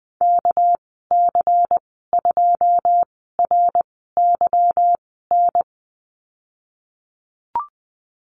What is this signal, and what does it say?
Morse code, heard as a keyed tone, about 20 wpm, spaced as for 15 wpm, 700 Hz. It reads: KC2RYN
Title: 238 Lesson 38g - Letter q - Call Signs 20wpm